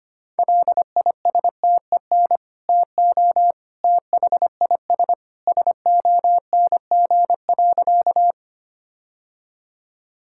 Morse code, heard as a keyed tone, about 25 wpm, 700 Hz. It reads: LISTEN TO THIS SONG.
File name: practice / 25wpm / Sentences with 2-Word Phrases - 25wpm